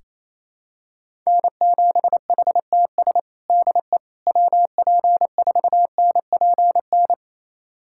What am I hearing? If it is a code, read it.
Morse code, heard as a keyed tone, about 28 wpm, 700 Hz. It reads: N7HTS DE WP4NPN
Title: USA Callsign DE Callsign 28wpm